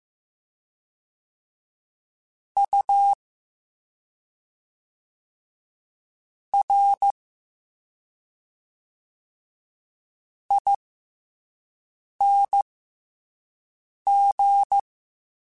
Morse code, heard as a keyed tone, about 15 wpm, spaced as for 5 wpm, 800 Hz. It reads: U R ING